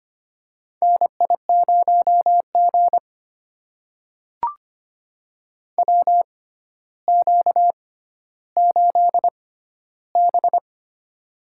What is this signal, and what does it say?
Morse code, heard as a keyed tone, about 25 wpm, spaced as for 13 wpm, 700 Hz. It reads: NI0G WQ8B
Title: USA Call Signs - 4 Characters - 6x Character Spacing - 25wpm